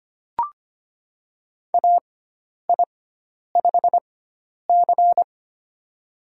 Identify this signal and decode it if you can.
Morse code, heard as a keyed tone, about 25 wpm, spaced as for 10 wpm, 700 Hz. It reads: AI5C
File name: Lesson 16g - Letter c - Call Signs 25wpm